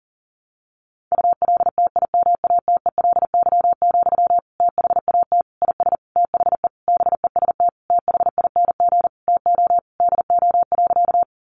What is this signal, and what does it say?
Morse code, heard as a keyed tone, about 40 wpm, 700 Hz. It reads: ULTIMATELY, THAT IS THE BEST THING TO DO.
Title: Sentences from Top 1200 Words 40wpm